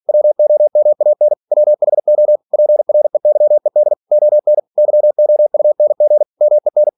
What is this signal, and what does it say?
Morse code, heard as a keyed tone, about 46 wpm, 600 Hz. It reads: WOMAN WHO PREYED ON YOUNG MEN